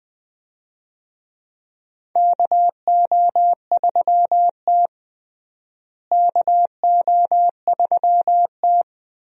Morse code, heard as a keyed tone, about 20 wpm, 700 Hz. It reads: KO3T KO3T
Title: USA Call Signs - 4 Characters - Repeated 2x - 20wpm